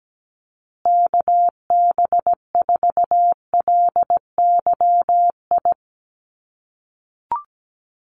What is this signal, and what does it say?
Morse code, heard as a keyed tone, about 17 wpm, 700 Hz. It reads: KB4LYI
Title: USA Call Signs Repeated 2x 17wpm